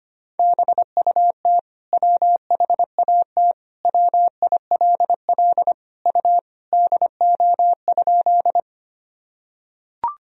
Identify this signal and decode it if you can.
Morse code, heard as a keyed tone, about 25 wpm, 700 Hz. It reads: BUT WHAT WILL U DO?
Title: Sentences from Top 100 Words and Abbreviations 25wpm